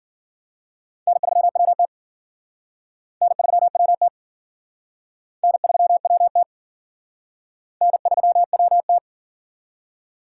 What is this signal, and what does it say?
Morse code, heard as a keyed tone, about 50 wpm, 700 Hz. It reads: N3WT N3WT N3WT N3WT
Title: Speed Racing - USA 1x2 Call Signs 40wpm